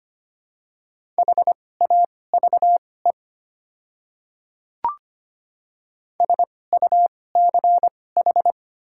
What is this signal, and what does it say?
Morse code, heard as a keyed tone, about 25 wpm, spaced as for 18 wpm, 700 Hz. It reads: HAVE SUCH
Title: 2x Character Spacing - Top 200 Words - 25wpm